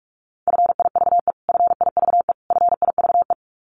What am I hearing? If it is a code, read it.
Morse code, heard as a keyed tone, about 44 wpm, 700 Hz. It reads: FIVE FIVE FIVE
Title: Top 500 Words Repeated 45wpm